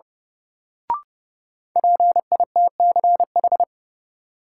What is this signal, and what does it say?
Morse code, heard as a keyed tone, about 32 wpm, 700 Hz. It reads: PITCH
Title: Sentences from Top 1300 Words - New Word Drill - 30wpm